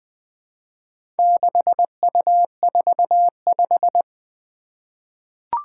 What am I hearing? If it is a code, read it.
Morse code, heard as a keyed tone, about 20 wpm, 700 Hz. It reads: 6U45
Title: Four Challenging Letter-Number Combinations 20wpm